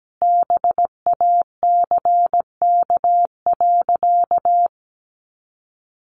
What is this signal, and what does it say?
Morse code, heard as a keyed tone, about 17 wpm, 700 Hz. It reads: BACK.
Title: Sentences from Top 300 Words 17wpm